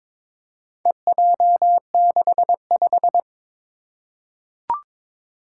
Morse code, heard as a keyed tone, about 22 wpm, 700 Hz. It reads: EJ65
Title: Four Challenging Letter-Number Combinations 22wpm